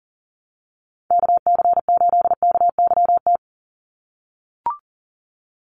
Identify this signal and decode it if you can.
Morse code, heard as a keyed tone, about 40 wpm, 700 Hz. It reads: KC8KYT
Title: USA Call Signs 40wpm